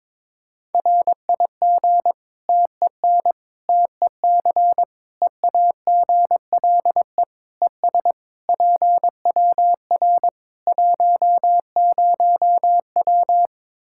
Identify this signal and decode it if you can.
Morse code, heard as a keyed tone, about 22 wpm, 700 Hz. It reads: RIG TEN TEC EAGLE ES PWR 10W
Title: K1USN SST QSO Practice 22wpm